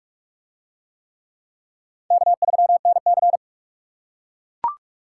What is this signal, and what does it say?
Morse code, heard as a keyed tone, about 42 wpm, 700 Hz. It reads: K3NC